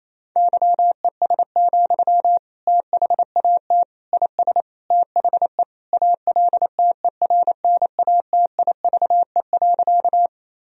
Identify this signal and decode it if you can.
Morse code, heard as a keyed tone, about 28 wpm, 700 Hz. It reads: YES, THAT IS THE ALTERNATIVE.
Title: Sentences from Top 1000 Words - Review Part 4 - 28wpm